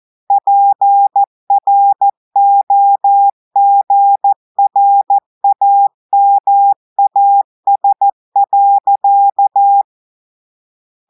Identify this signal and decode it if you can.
Morse code, heard as a keyed tone, about 14 wpm, 800 Hz. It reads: PROGRAMAS.
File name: audios / portugais / portugais texte en clair 2 vitesse 14 mots minutes